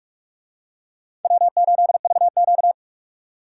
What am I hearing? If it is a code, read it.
Morse code, heard as a keyed tone, about 43 wpm, 700 Hz. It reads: W8VQ